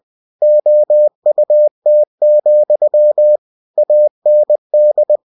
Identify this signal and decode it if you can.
Morse code, heard as a keyed tone, about 20 wpm, 600 Hz. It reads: OUT, AND